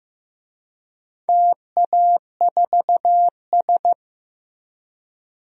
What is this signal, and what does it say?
Morse code, heard as a keyed tone, about 15 wpm, 700 Hz. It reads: TA4S
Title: DXCC-100 Call Signs - 4 Characters - Repeated 3x - 15wpm